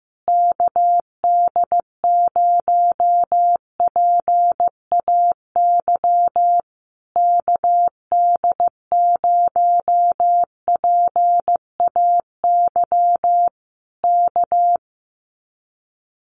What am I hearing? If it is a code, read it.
Morse code, heard as a keyed tone, about 15 wpm, 700 Hz. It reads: KD0PAY KD0PAY K
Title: Common QSO Phrases Encore 15wpm